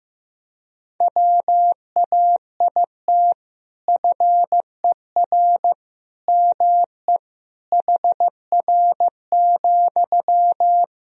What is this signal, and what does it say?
Morse code, heard as a keyed tone, about 15 wpm, 700 Hz. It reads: WAIT FER ME HR,
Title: Sentences from Top 500 Words and Abbreviations 15wpm